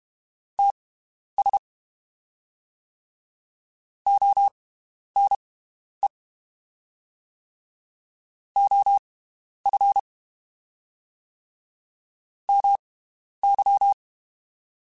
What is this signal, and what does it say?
Morse code, heard as a keyed tone, about 32 wpm, spaced as for 11 wpm, 800 Hz. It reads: TS ONE OF MY
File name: (spaced) / QSOs Spaced-mp3s / WD0FBT 1 @30 sp